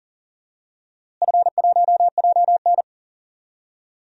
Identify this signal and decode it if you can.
Morse code, heard as a keyed tone, about 40 wpm, 700 Hz. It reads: F1JN
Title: DXCC-100 Call Signs - 4 Characters - Repeated 3x - 40wpm